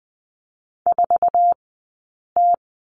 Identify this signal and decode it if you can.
Morse code, heard as a keyed tone, about 20 wpm, spaced as for 8 wpm, 700 Hz. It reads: 4T
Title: 165 Lesson 27d - Number 7 - Two Characters 20wpm